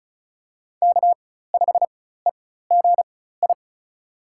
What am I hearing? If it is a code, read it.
Morse code, heard as a keyed tone, about 34 wpm, spaced as for 16 wpm, 700 Hz. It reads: K5EGI
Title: USA Call Signs - 5 Characters - 4x Character Spacing - 35wpm